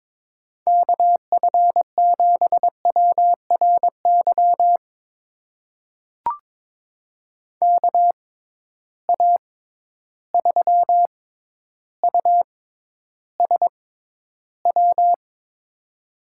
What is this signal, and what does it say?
Morse code, heard as a keyed tone, about 22 wpm, spaced as for 11 wpm, 700 Hz. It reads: KF7WRY KA3USW
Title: USA Call Signs - 6 Characters - 6x Character Spacing - 22wpm